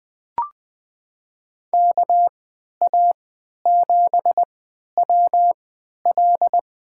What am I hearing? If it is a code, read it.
Morse code, heard as a keyed tone, about 21 wpm, spaced as for 11 wpm, 700 Hz. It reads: KA7WL
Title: USA Call Signs - 3x Character Spacing - 20wpm